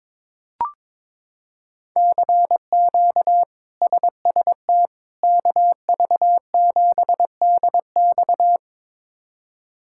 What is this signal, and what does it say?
Morse code, heard as a keyed tone, about 22 wpm, 700 Hz. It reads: CQ SST KV7DX
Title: K1USN SST QSO Practice - 22wpm